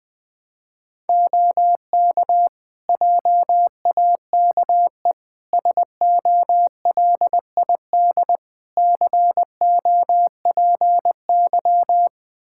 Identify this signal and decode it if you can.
Morse code, heard as a keyed tone, about 20 wpm, 700 Hz. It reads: OK JAKE SOLID COPY
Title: Practice QSO - Exchange 3 - More Info - 20wpm